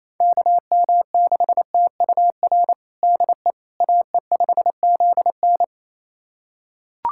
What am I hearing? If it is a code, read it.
Morse code, heard as a keyed tone, about 27 wpm, 700 Hz. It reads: KM6TUR DE AE5ZN
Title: USA Callsign DE Callsign 28wpm